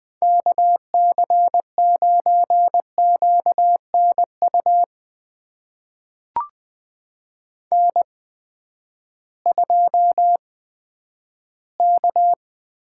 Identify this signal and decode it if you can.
Morse code, heard as a keyed tone, about 20 wpm, spaced as for 11 wpm, 700 Hz. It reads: KC9QNU N2K
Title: USA Call Signs - 8x Character Spacing - 20wpm